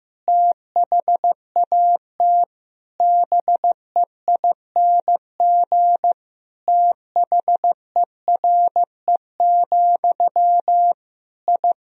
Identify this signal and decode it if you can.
Morse code, heard as a keyed tone, about 16 wpm, 700 Hz. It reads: THAT BEING THERE, I
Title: Sentences from Top 1000 Words - Review Part 3 - 15wpm